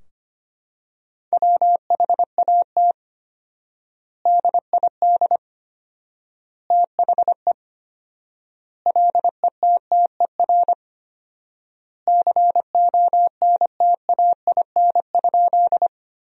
Morse code, heard as a keyed tone, about 25 wpm, 700 Hz. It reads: WHAT DID THE LETTER CONTAIN?
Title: 4x Word-Spacing - Sentences from Top 500 Words - 25wpm